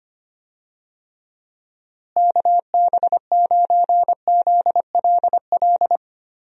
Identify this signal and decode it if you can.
Morse code, heard as a keyed tone, about 25 wpm, 700 Hz. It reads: KB9ZLL
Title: USA Call Signs - 6 Characters - Repeated 3x - 25wpm